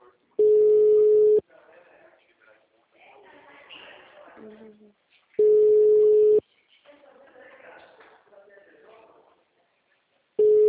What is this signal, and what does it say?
Ringback tone (Europe, 425 Hz), with speech.